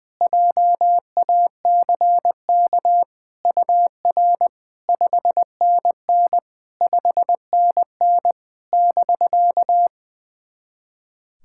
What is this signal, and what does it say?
Morse code, heard as a keyed tone, about 20 wpm, 700 Hz. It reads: JACK UR 5NN 5NN <BK>
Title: SOTA QSO Practice 20wpm